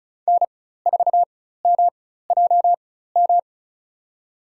Morse code, heard as a keyed tone, about 36 wpm, spaced as for 16 wpm, 700 Hz. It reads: N4MJM
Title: USA Call Signs - 5 Characters - 4x Character Spacing - 35wpm